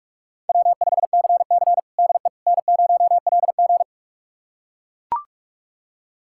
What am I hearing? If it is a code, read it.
Morse code, heard as a keyed tone, about 42 wpm, 700 Hz. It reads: W5CC DE N0LG
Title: USA Callsign DE Callsign - 4 Characters - 45wpm